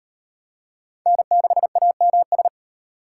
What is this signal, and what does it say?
Morse code, heard as a keyed tone, about 38 wpm, 700 Hz. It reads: N6AMS